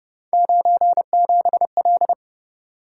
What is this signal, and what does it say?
Morse code, heard as a keyed tone, about 30 wpm, 700 Hz. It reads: 97L